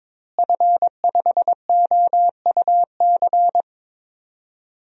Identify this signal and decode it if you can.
Morse code, heard as a keyed tone, about 21 wpm, 700 Hz. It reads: F5OUC
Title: DXCC-100 Call Signs - 5 Characters - Repeated 2x - 22wpm